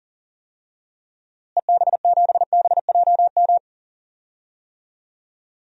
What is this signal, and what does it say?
Morse code, heard as a keyed tone, about 37 wpm, 700 Hz. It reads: EB7BJM
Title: DXCC-100 Call Signs - 6 Characters - Repeated 3x - 40wpm